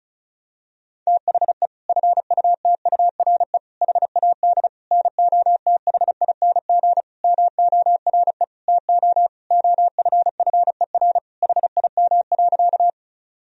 Morse code, heard as a keyed tone, about 36 wpm, 700 Hz. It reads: THE FUTURE HAD NOTHING MORE TO OFFER HIM.
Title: Sentences from Top 1000 Words - Review Part 4 - 35wpm